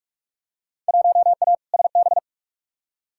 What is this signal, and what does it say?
Morse code, heard as a keyed tone, about 47 wpm, 700 Hz. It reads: 1A SB
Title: Field Day Exchanges Encore - 45wpm